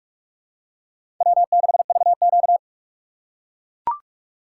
Morse code, heard as a keyed tone, about 43 wpm, 700 Hz. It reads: W6VQ
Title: USA Call Signs - 4 Characters - 4x Character Spacing - 45wpm